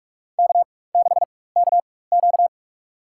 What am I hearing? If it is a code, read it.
Morse code, heard as a keyed tone, about 45 wpm, spaced as for 21 wpm, 690 Hz. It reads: K6KQ
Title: USA Call Signs - 4 Characters - 4x Character Spacing - 45wpm